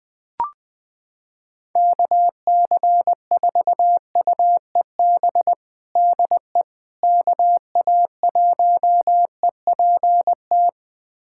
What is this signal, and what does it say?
Morse code, heard as a keyed tone, about 20 wpm, 700 Hz. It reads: KC4UEB DE KA1EPT
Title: USA Callsign DE Callsign - 6 Characters - 20wpm